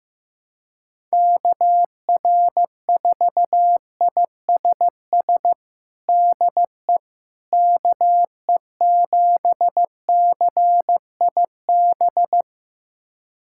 Morse code, heard as a keyed tone, about 16 wpm, 700 Hz. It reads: KR4ISS DE KE7CIB
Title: USA Callsign DE Callsign - 6 Characters - 15wpm